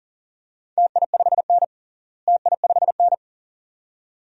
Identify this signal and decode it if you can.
Morse code, heard as a keyed tone, about 39 wpm, 700 Hz. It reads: TI5N TI5N
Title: DXCC-100 Call Signs - 4 Characters - Repeated 2x - 40wpm